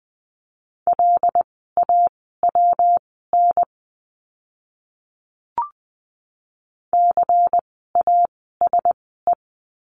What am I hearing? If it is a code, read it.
Morse code, heard as a keyed tone, about 20 wpm, spaced as for 14 wpm, 700 Hz. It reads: LAWN CASE
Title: 115 Lesson 19f - Letter w - Words 20wpm